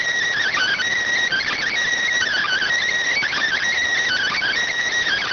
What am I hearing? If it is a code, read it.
Emergency Alert System (SAME) tones, heard as harsh buzzing bursts. It is heard from 0.4 to 4.6 s.